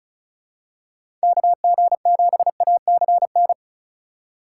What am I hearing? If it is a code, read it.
Morse code, heard as a keyed tone, about 34 wpm, 700 Hz. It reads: KG7ACN
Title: USA Call Signs - 6 Characters - 3x Character Spacing - 35wpm